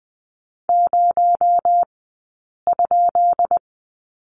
Morse code, heard as a keyed tone, about 20 wpm, 700 Hz. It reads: 0 ?
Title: 217 Lesson 35d - Number 0 - Two Characters 20wpm